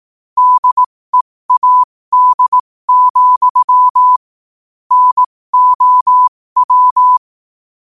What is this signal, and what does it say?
Morse code, heard as a keyed tone, about 18 wpm, spaced as for 15 wpm, 1000 Hz. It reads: DEAD, NOW
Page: New quotes every day in morse code at 15 Words per minute.